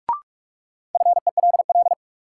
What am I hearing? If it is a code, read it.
Morse code, heard as a keyed tone, about 42 wpm, 700 Hz. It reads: FELL
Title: Sets of 5 Words Encore - New Word Drill - 45wpm